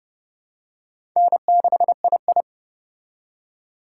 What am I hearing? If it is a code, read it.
Morse code, heard as a keyed tone, about 28 wpm, 700 Hz. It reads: N6II